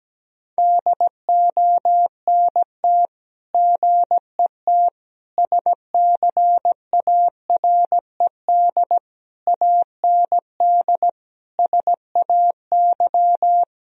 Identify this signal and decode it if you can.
Morse code, heard as a keyed tone, about 17 wpm, 700 Hz. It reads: DONT GET SCARED AND SAY
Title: Sentences from Top 1000 Words - Review Part 4 - 17wpm